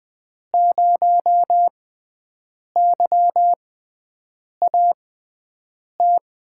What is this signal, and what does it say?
Morse code, heard as a keyed tone, about 20 wpm, spaced as for 7 wpm, 700 Hz. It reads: 0YAT
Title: USA Call Signs - 5 Characters - 6x Character Spacing - 20wpm